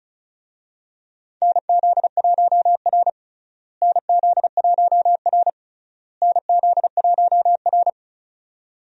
Morse code, heard as a keyed tone, about 36 wpm, 700 Hz. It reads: NZ1R NZ1R NZ1R
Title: USA Call Signs - 4 Characters - Repeated 3x - 35wpm